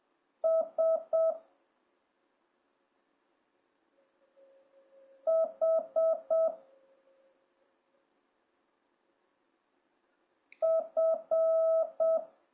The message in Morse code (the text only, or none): S H F